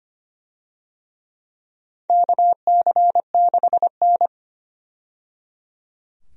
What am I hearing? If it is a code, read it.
Morse code, heard as a keyed tone, about 25 wpm, 700 Hz. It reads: KC6N
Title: USA Call Signs - 4 Characters - 25wpm